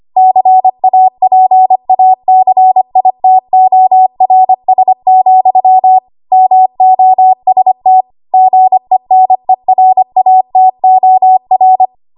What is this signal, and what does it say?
Morse code, heard as a keyed tone, about 25 wpm, 750 Hz. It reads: CAPACITORS, MOST GENERATOR
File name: Archive / 25 WPM / 25WPM